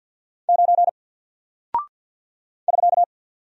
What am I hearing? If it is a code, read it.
Morse code, heard as a keyed tone, about 48 wpm, 700 Hz. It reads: 9 <SK>
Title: Single Character (Letter:Number:Punctuation:Prosign) Alt 50wpm